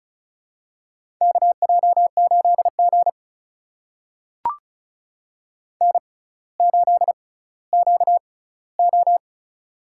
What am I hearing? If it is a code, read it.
Morse code, heard as a keyed tone, about 36 wpm, spaced as for 18 wpm, 700 Hz. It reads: KJ8G N8QO
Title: USA Call Signs - 4 Characters - 6x Character Spacing - 35wpm